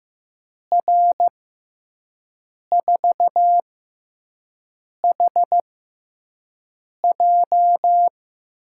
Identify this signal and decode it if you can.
Morse code, heard as a keyed tone, about 15 wpm, spaced as for 5 wpm, 700 Hz. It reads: R4HJ
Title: USA Call Signs - 5 Characters - 6x Character Spacing - 15wpm